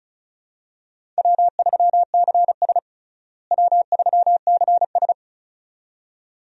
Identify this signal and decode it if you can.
Morse code, heard as a keyed tone, about 36 wpm, 700 Hz. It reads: W3CS W3CS
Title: USA Call Signs - 4 Characters - Repeated 2x - 35wpm